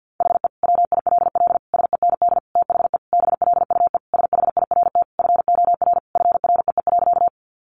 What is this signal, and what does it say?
Morse code, heard as a keyed tone, about 50 wpm, 700 Hz. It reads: HE WILL SEND THE BLUE SHIRT FOR FREE.
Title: Additional Easy Phrases 50wpm